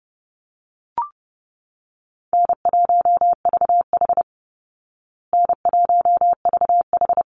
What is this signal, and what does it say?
Morse code, heard as a keyed tone, about 28 wpm, 700 Hz. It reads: N1VH N1VH
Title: Call Signs Repeated 3x - 30wpm